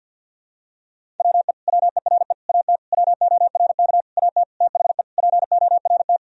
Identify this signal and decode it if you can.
Morse code, heard as a keyed tone, about 51 wpm, 700 Hz. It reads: WE WERE AT WORK AT THE PORT
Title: Sentences from Top 1000 Words - Review Part 1 - 50wpm